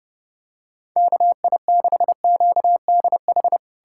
Morse code, heard as a keyed tone, about 32 wpm, 700 Hz. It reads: KI6QDH
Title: K1USN SST QSO Practice - 30wpm